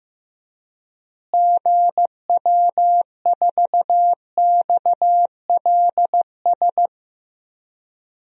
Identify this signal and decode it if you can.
Morse code, heard as a keyed tone, about 16 wpm, 700 Hz. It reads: GW4XLS
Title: DXCC-100 Call Signs - 6 Characters - Repeated 3x - 15wpm